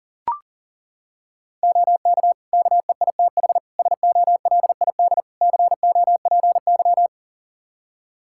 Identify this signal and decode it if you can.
Morse code, heard as a keyed tone, about 37 wpm, 700 Hz. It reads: OK KEITH SOLID COPY
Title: Practice QSO - Exchange 3 - More Info - 40wpm